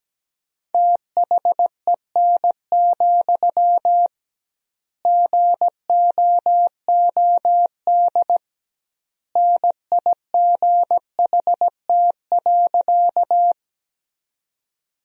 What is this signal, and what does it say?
Morse code, heard as a keyed tone, about 17 wpm, 700 Hz. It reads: THEN, GOOD NIGHT.
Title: 2x Word-Spacing - Sentences from Top 300 Words - 17wpm